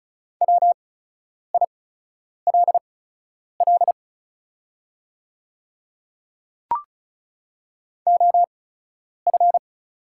Morse code, heard as a keyed tone, about 36 wpm, spaced as for 10 wpm, 690 Hz. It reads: WILL OF
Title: 8x Character Spacing - Top 200 Words - 35wpm